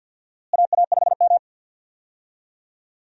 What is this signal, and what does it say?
Morse code, heard as a keyed tone, about 51 wpm, 700 Hz. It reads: AA5M
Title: USA Call Signs - 4 Characters - Repeated 2x - 50wpm